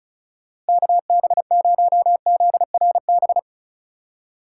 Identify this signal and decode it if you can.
Morse code, heard as a keyed tone, about 36 wpm, 700 Hz. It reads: KB0ZRB
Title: USA Call Signs - 6 Characters - 4x Character Spacing - 35wpm